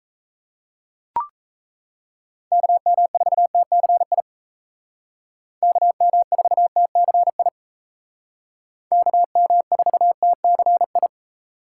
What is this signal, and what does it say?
Morse code, heard as a keyed tone, about 36 wpm, 700 Hz. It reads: KM4TCI KM4TCI KM4TCI